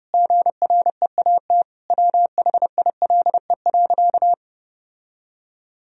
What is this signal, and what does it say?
Morse code, heard as a keyed tone, about 30 wpm, 700 Hz. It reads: GREAT WHILE.
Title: Sentences from Top 1000 Words - Review Part 1 - 30wpm